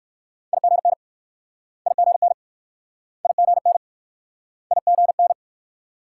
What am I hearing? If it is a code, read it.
Morse code, heard as a keyed tone, about 58 wpm, 690 Hz. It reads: IGN IGN IGN IGN